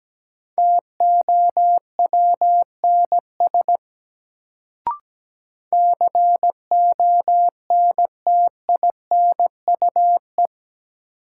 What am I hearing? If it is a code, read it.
Morse code, heard as a keyed tone, about 17 wpm, 700 Hz. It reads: TOWNS CONTINUE
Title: Sentences from Top 1000 Words - Review Part 6 - New Word Drill - 17wpm